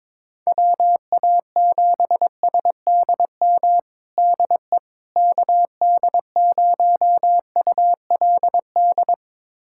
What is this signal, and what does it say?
Morse code, heard as a keyed tone, about 22 wpm, 700 Hz. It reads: WA7SDM DE KD0ULD
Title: USA Callsign DE Callsign - 6 Characters - 22wpm